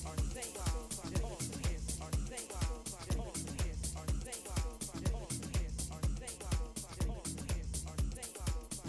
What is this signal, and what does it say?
There is no code.